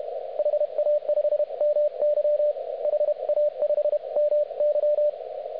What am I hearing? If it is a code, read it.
Morse code, heard as a keyed tone, about 31 wpm, 590 Hz, with radio noise. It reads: HA5MY HA5MY